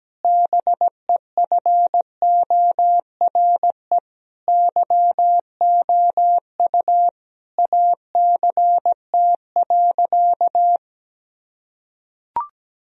Morse code, heard as a keyed tone, about 17 wpm, 700 Hz. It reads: BEFORE YOU ACT.
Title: Sentences from Top 1000 Words - Review Part 1 - 17wpm